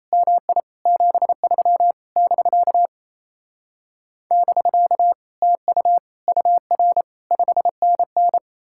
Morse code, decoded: MI 73 <BK> <BK> TU UR 5NN

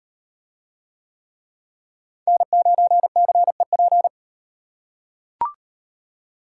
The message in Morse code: N9CEP

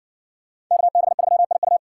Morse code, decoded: KB3IU

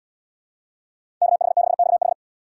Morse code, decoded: KH6XH

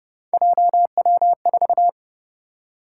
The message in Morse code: JW4

